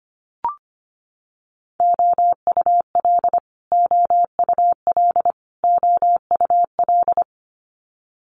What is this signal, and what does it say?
Morse code, heard as a keyed tone, about 25 wpm, 700 Hz. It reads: OUL OUL OUL
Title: Multi-letter Phonemes Repeated 3x 25wpm